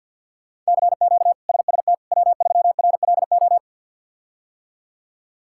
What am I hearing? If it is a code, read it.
Morse code, heard as a keyed tone, about 53 wpm, 700 Hz. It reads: CQ SST W3RLO
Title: K1USN SST QSO Practice - 50wpm